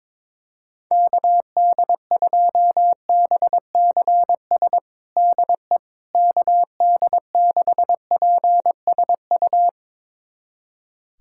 Morse code, heard as a keyed tone, about 22 wpm, 700 Hz. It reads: KD2BCS DE KD6PSU
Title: USA Callsign DE Callsign 22wpm